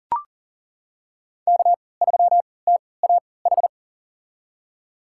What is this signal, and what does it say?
Morse code, heard as a keyed tone, about 40 wpm, spaced as for 23 wpm, 700 Hz. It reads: K3TAH
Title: USA Call Signs - 3x Character Spacing - 40wpm